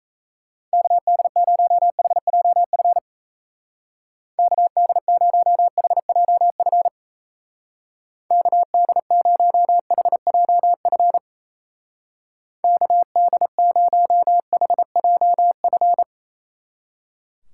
Morse code, heard as a keyed tone, about 35 wpm, 700 Hz. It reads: KD0HJF KD0HJF KD0HJF KD0HJF
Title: Speed Racing - USA 2x3 Call Signs 28wpm